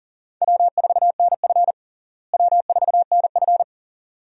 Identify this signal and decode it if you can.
Morse code, heard as a keyed tone, about 40 wpm, 700 Hz. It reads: W4NF W4NF
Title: USA Call Signs - 4 Characters - Repeated 2x - 40wpm